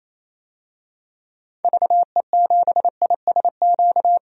Morse code, heard as a keyed tone, about 28 wpm, 700 Hz. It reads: VE7ISQ
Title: DXCC-100 Call Signs - 6 Characters - 28wpm